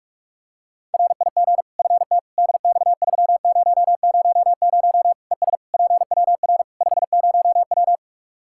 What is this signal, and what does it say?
Morse code, heard as a keyed tone, about 41 wpm, 700 Hz. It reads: RIG FT DX3000 ES PWR 50W
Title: Practice QSO - Exchange 2 - Info - 45wpm